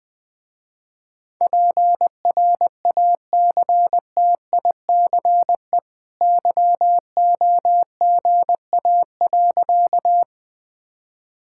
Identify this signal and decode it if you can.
Morse code, heard as a keyed tone, about 20 wpm, 700 Hz. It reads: PRACTICE YOGA.
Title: Easy Sentences with Hints - 20wpm